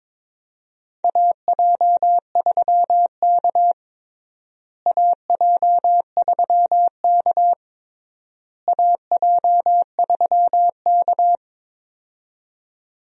Morse code, heard as a keyed tone, about 22 wpm, 700 Hz. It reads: AJ3K AJ3K AJ3K
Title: USA Call Signs - 4 Characters - Repeated 3x - 22wpm